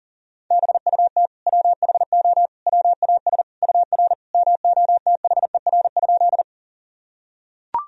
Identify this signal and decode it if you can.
Morse code, heard as a keyed tone, about 40 wpm, 700 Hz. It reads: BUT WHO WAS UR MOTHER?
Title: Sentences from Top 200 Words and Abbreviations 40wpm